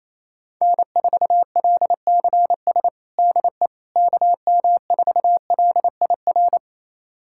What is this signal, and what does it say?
Morse code, heard as a keyed tone, about 28 wpm, 700 Hz. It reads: N4LCS DE KM4LIR
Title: USA Callsign DE Callsign 28wpm